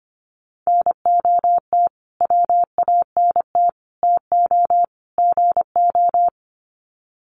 Morse code, heard as a keyed tone, about 25 wpm, 700 Hz. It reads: NOT WANT TO GO
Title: Sets of 4 Words 25wpm